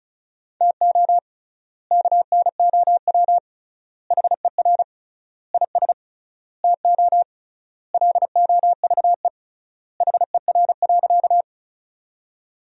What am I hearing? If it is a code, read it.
Morse code, heard as a keyed tone, about 36 wpm, 700 Hz. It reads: TO KNOW HER IS TO LOVE HER.
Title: 3x Word-Spacing - Sentences from Top 400 Words - 35wpm